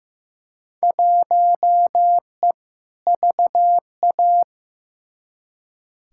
Morse code, heard as a keyed tone, about 15 wpm, 700 Hz. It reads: 1E VA
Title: Field Day Exchanges - 15wpm